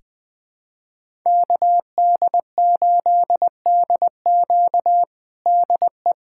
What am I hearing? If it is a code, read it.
Morse code, heard as a keyed tone, about 19 wpm, 700 Hz. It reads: KD8DQ DE
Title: USA Callsign DE Callsign - 5 Characters - 20wpm